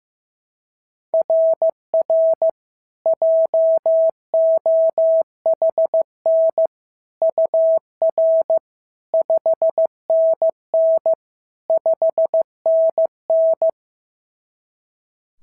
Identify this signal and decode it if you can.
Morse code, heard as a keyed tone, about 15 wpm, 690 Hz. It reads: RR JOHN UR 5NN 5NN
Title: SOTA QSO Practice 15wpm